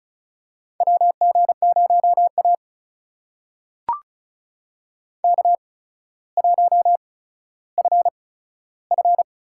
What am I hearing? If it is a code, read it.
Morse code, heard as a keyed tone, about 36 wpm, spaced as for 15 wpm, 700 Hz. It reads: WG0A K1FF
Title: USA Call Signs - 4 Characters - 8x Character Spacing - 35wpm